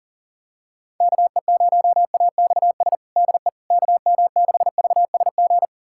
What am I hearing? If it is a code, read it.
Morse code, heard as a keyed tone, about 40 wpm, 700 Hz. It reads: KE0AXS DE KM6VSG
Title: USA Callsign DE Callsign 40wpm